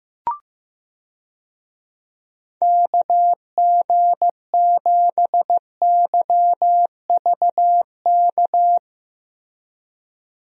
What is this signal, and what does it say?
Morse code, heard as a keyed tone, about 15 wpm, 700 Hz. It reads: KG7YVK
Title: USA Call Signs - 6 Characters - Repeated 2x - 15wpm